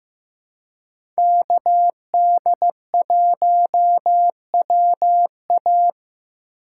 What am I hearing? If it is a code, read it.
Morse code, heard as a keyed tone, about 15 wpm, 700 Hz. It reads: KD1WA
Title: USA Call Signs - 5 Characters - 4x Character Spacing - 15wpm